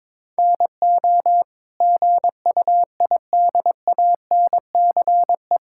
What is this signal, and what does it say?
Morse code, heard as a keyed tone, about 22 wpm, 700 Hz. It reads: NO GUIDANCE
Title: Sentences from Top 1600 Words 22wpm